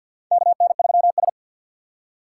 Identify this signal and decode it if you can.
Morse code, heard as a keyed tone, about 51 wpm, 700 Hz. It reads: KN3S
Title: USA Call Signs - 4 Characters - 6x Character Spacing - 50wpm